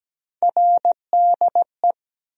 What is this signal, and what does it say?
Morse code, heard as a keyed tone, about 17 wpm, 700 Hz. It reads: RDE
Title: USA Call Sign Postfixes - 3 Letters 17wpm